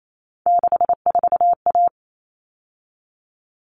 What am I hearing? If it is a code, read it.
Morse code, heard as a keyed tone, about 28 wpm, 700 Hz. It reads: 64A
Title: All Cut Numbers - 3-Digit 28wpm